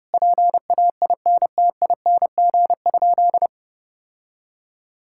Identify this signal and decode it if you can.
Morse code, heard as a keyed tone, about 30 wpm, 700 Hz. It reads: PAINTING?
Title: Sentences from Top 1000 Words - Review Part 6 - 30wpm